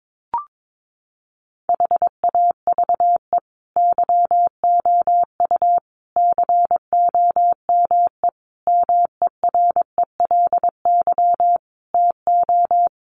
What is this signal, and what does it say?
Morse code, heard as a keyed tone, about 22 wpm, 700 Hz. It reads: HAVE YOU COME MERELY TO
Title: Sentences from Top 1400 Words 22wpm